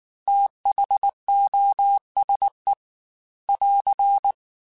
Morse code, heard as a keyed tone, about 19 wpm, 800 Hz. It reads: THOSE <AR>
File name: T then H Runs / T then H Runs.mp3 / T then H Runs @18